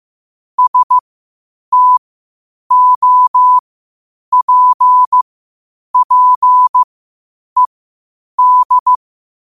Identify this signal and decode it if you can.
Morse code, heard as a keyed tone, about 15 wpm, spaced as for 8 wpm, 1000 Hz. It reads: STOPPED
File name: qotdfiles / 8wpm